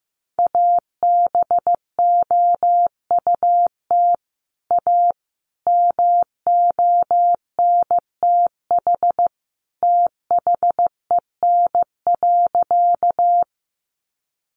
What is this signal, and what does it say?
Morse code, heard as a keyed tone, about 15 wpm, 700 Hz. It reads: ABOUT A MONTH THEN.
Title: Sentences from Top 800 Words 15wpm